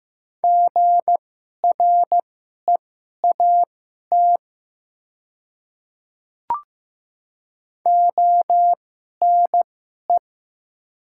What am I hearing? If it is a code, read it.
Morse code, heard as a keyed tone, about 16 wpm, spaced as for 11 wpm, 700 Hz. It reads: GREAT ONE
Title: 2x Character Spacing - Top 300 Words - 15wpm